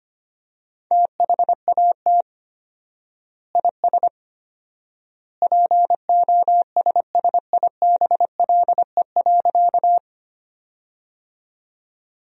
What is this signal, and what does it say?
Morse code, heard as a keyed tone, about 25 wpm, 700 Hz. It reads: THAT IS POSSIBLE.
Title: 4x Word-Spacing - Sentences from Top 500 Words - 25wpm